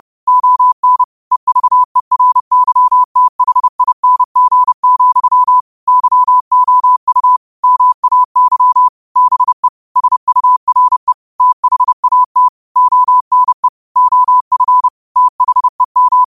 Morse code, heard as a keyed tone, about 30 wpm, 1000 Hz. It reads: ON EVERYTHING, YOU MAY BE SURE THAT ONE OF THEM
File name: qotdfiles / 30wpm